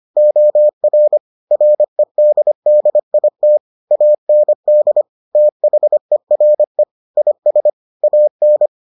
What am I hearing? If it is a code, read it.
Morse code, heard as a keyed tone, about 25 wpm, 600 Hz. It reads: OR REDDIT AND THERE IS AN